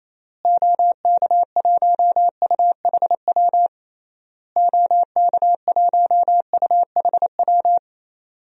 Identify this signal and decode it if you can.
Morse code, heard as a keyed tone, about 28 wpm, 700 Hz. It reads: OK1UHW OK1UHW
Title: DXCC-100 Call Signs - 6 Characters - Repeated 3x - 28wpm